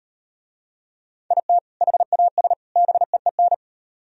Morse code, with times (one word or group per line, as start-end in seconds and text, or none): 1.3-1.6 IT
1.8-2.5 HAS
2.8-3.5 BEEN